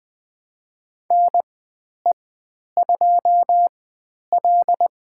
Morse code, heard as a keyed tone, about 20 wpm, spaced as for 10 wpm, 700 Hz. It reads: NE2L
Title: 109 Lesson 18g - Letter m - Call Signs 20wpm